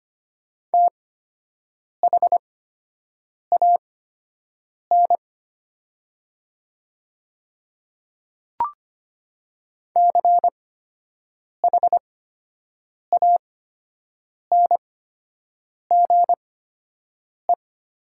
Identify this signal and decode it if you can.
Morse code, heard as a keyed tone, about 25 wpm, spaced as for 7 wpm, 700 Hz. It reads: THAN CHANGE
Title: 8x Character Spacing - Top 300 Words - 25wpm